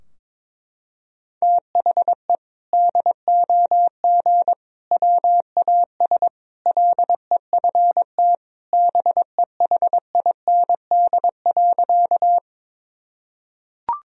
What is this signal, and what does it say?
Morse code, heard as a keyed tone, about 22 wpm, 700 Hz. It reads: THE DOG WAS LEFT BEHIND.
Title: Sentences from Top 1000 Words - Review Part 2 - 22wpm